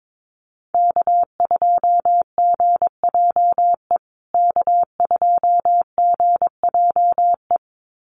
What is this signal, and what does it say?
Morse code, heard as a keyed tone, about 22 wpm, 700 Hz. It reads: K2GJE K2GJE
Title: USA Call Signs Repeated 2x 22wpm